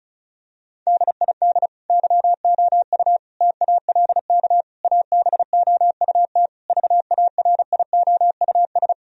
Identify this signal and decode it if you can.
Morse code, heard as a keyed tone, about 36 wpm, 700 Hz. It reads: DID YOU TALK ABOUT VARIOUS
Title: Sentences from Top 1000 Words - Review Part 4 - 35wpm